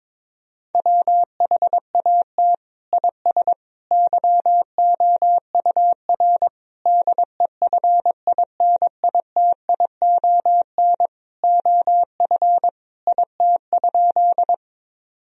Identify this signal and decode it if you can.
Morse code, heard as a keyed tone, about 22 wpm, 700 Hz. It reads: WHAT IS YOUR DEFINITION OF IT?
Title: Sentences from Top 1000 Words - Review Part 4 - 22wpm